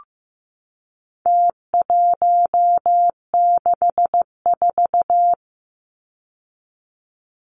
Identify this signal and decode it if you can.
Morse code, heard as a keyed tone, about 15 wpm, 700 Hz. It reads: T164
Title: Numbers - Contest Style - 4-Digit 15wpm